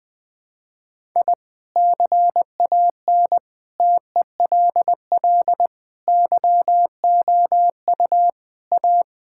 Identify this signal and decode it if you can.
Morse code, heard as a keyed tone, about 20 wpm, 700 Hz. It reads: I CAN TELL YOU A
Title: Sentences with 4-Word Phrases - 20wpm